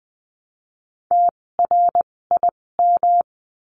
Morse code, heard as a keyed tone, about 20 wpm, spaced as for 16 wpm, 700 Hz. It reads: TRIM
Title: 256 Lesson 42a - Words 12-20wpm